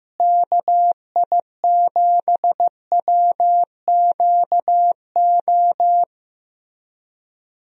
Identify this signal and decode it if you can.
Morse code, heard as a keyed tone, about 15 wpm, 700 Hz. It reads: KI7WQO
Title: USA Call Signs - 6 Characters - Repeated 3x - 15wpm